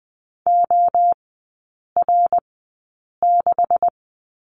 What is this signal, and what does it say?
Morse code, heard as a keyed tone, about 20 wpm, spaced as for 8 wpm, 700 Hz. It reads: OR6
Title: 218 Lesson 35e - Number 0 - Three Characters 20wpm